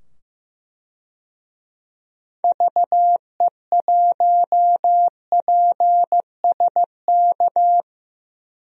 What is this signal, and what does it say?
Morse code, heard as a keyed tone, about 15 wpm, 700 Hz. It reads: VE1PSK
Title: DXCC-100 Call Signs - 6 Characters - 15wpm